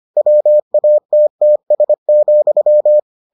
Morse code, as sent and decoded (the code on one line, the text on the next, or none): .-- .- - - ... --..--
WATTS,